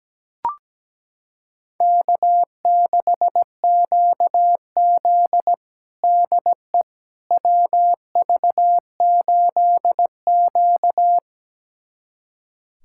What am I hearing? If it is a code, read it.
Morse code, heard as a keyed tone, about 17 wpm, 700 Hz. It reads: K6QZ DE WV8Q
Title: USA Callsign DE Callsign - 4 Characters - 17wpm